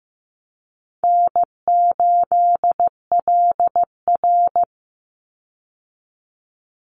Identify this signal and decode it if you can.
Morse code, heard as a keyed tone, about 15 wpm, 700 Hz. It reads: N8LR
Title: Easy Call Signs - 15wpm